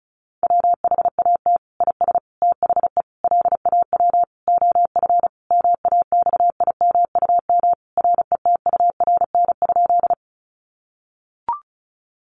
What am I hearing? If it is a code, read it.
Morse code, heard as a keyed tone, about 34 wpm, 700 Hz. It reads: WHAT IS THE LAW OF MAXIMUM RETURN?
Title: Sentences from Top 1300 Words 35wpm